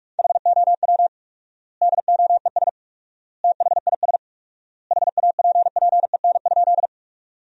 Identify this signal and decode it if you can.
Morse code, heard as a keyed tone, about 46 wpm, 700 Hz. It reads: HOW DOES THIS HAPPEN?
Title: 4x Word-Spacing - Sentences from Top 400 Words - 45wpm